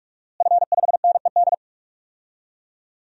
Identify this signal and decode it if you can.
Morse code, heard as a keyed tone, about 43 wpm, 700 Hz. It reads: F5NED
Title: DXCC-100 Call Signs - 5 Characters - Repeated 3x - 45wpm